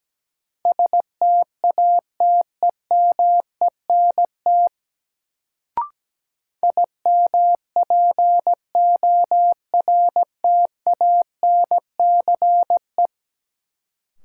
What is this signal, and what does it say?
Morse code, heard as a keyed tone, about 17 wpm, 700 Hz. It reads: STATEMENT IMPORTANCE
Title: Sentences from Top 1200 Words - New Word Drill Alt - 17wpm